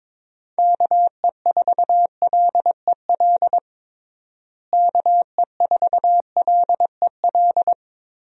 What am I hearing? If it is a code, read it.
Morse code, heard as a keyed tone, about 22 wpm, 700 Hz. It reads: KE4LEL KE4LEL
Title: USA Call Signs - 6 Characters - Repeated 3x - 22wpm